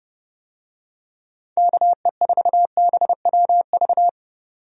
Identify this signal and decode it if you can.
Morse code, heard as a keyed tone, about 30 wpm, 700 Hz. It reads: KE4BWV
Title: USA Call Signs - 6 Characters - 30wpm